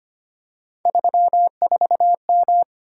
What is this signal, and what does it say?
Morse code, heard as a keyed tone, about 25 wpm, 700 Hz. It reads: 34M
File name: Three Letter-Number Combinations - 25wpm